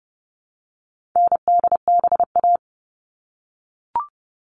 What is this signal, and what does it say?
Morse code, heard as a keyed tone, about 30 wpm, 700 Hz. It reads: NDBA
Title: All Cut Numbers - 4-Digit 30wpm